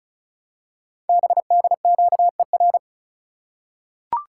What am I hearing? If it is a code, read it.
Morse code, heard as a keyed tone, about 36 wpm, 690 Hz. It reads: BDQER
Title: Code Groups - 5 Letters - 35wpm